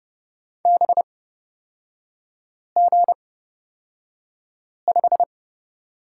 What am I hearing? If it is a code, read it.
Morse code, heard as a keyed tone, about 30 wpm, spaced as for 5 wpm, 700 Hz. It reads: BG5
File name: Lesson 37e - Letter x - Three Characters 30wpm